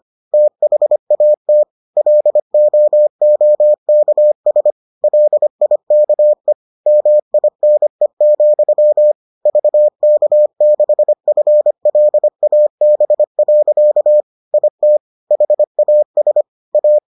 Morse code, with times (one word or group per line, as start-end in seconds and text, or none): 0.3-1.6 THAT
2.0-4.7 LOOKS
5.0-6.5 LIKE
6.9-9.1 MINE,
9.5-14.2 VK6FLAB.
14.5-15.0 IT
15.3-16.4 HAS
16.7-17.0 A